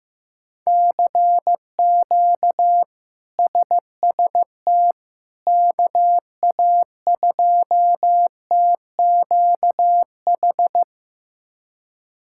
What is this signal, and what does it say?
Morse code, heard as a keyed tone, about 15 wpm, 700 Hz. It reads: CQ SST KA2TQH
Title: K1USN SST QSO Practice - 15wpm